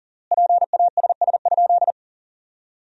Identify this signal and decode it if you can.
Morse code, heard as a keyed tone, about 37 wpm, 700 Hz. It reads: PASS?
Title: 3x Word-Spacing - Sentences from Top 400 Words - 40wpm